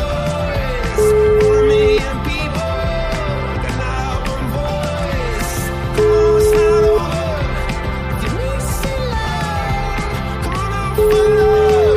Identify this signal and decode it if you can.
Ringback tone (Europe, 425 Hz), with noise.